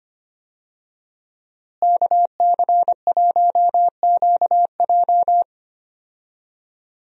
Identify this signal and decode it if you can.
Morse code, heard as a keyed tone, about 25 wpm, 700 Hz. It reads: KC1QJ
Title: USA Call Signs - 5 Characters - Repeated 2x - 25wpm